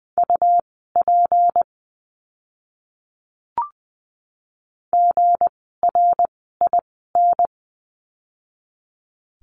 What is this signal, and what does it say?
Morse code, heard as a keyed tone, about 20 wpm, spaced as for 14 wpm, 700 Hz. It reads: UP GRIN
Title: 160 Lesson 26f - Letter g - Words 20wpm